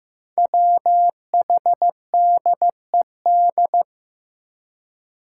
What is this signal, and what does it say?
Morse code, heard as a keyed tone, about 16 wpm, 700 Hz. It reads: WHDED
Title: Five Challenging Letter-Number Combinations 15wpm